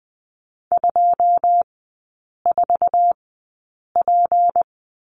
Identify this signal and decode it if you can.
Morse code, heard as a keyed tone, about 20 wpm, spaced as for 8 wpm, 700 Hz. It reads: 24P
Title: 218 Lesson 35e - Number 0 - Three Characters 20wpm